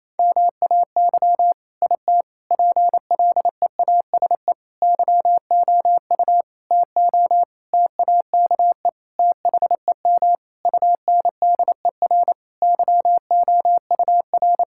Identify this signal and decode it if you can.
Morse code, heard as a keyed tone, about 28 wpm, 700 Hz. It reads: MAY IT PLEASE YOU TO TAKE THEM UNDER YOUR
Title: Sentences from Top 1600 Words 28wpm